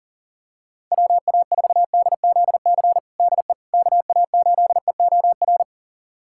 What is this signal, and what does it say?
Morse code, heard as a keyed tone, about 40 wpm, 700 Hz. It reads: WA4DZC DE KA8EOR
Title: USA Callsign DE Callsign - 6 Characters - 40wpm